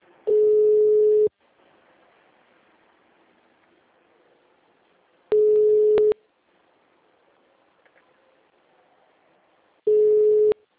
Ringback tone (Europe, 425 Hz).
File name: out- -2006- - - .53543